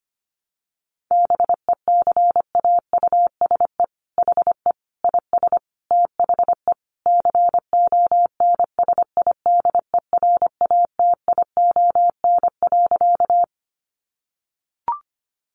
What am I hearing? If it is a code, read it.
Morse code, heard as a keyed tone, about 25 wpm, 700 Hz. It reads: BECAUSE HE IS THE CONSIDERATION.
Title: Sentences from Top 1500 Words 25wpm